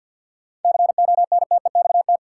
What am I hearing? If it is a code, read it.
Morse code, heard as a keyed tone, about 51 wpm, 700 Hz. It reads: CONTEXT
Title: Sets of 4 Words - New Word Drill - 50wpm